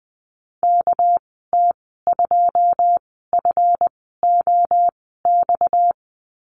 Morse code, heard as a keyed tone, about 20 wpm, spaced as for 14 wpm, 700 Hz. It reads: KT2FOX
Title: 245 Lesson 39g - Letter z - Call Signs 20wpm